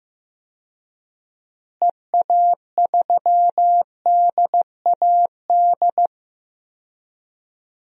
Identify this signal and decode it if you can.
Morse code, heard as a keyed tone, about 16 wpm, 700 Hz. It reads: EA3DAD
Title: DXCC-100 Call Signs - 6 Characters - Repeated 3x - 15wpm